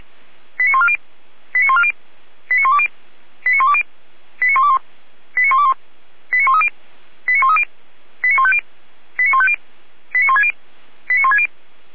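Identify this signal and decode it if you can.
Five-tone selective call (ZVEI), heard as quick runs of five tones, tones of about 70 ms. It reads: 89140 89140 89120 89120 89121 89121 89130 89130 89150 89150 89160 89160
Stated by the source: ZVEI five-tone selective call 89119, received via SDR.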